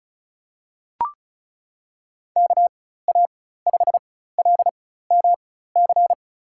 Morse code, heard as a keyed tone, about 36 wpm, spaced as for 16 wpm, 700 Hz. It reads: KA5LMC
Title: USA Call Signs - 6 Characters - 4x Character Spacing - 35wpm